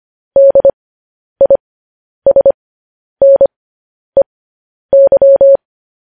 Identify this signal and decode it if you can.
Morse code, heard as a keyed tone, about 25 wpm, spaced as for 10 wpm, 550 Hz. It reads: DISNEY
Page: news headlines in morse code at 10 wpm